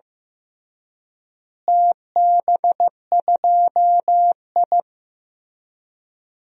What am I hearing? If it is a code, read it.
Morse code, heard as a keyed tone, about 15 wpm, 700 Hz. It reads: TB2I
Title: DXCC-100 Call Signs - 4 Characters - Repeated 3x - 15wpm